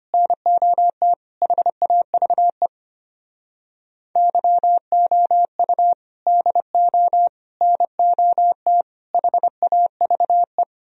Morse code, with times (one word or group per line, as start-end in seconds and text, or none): 0.1-1.1 NOT
1.4-2.7 HAVE
4.2-5.9 YOU
6.3-7.3 DO
7.6-8.8 NOT
9.1-10.6 HAVE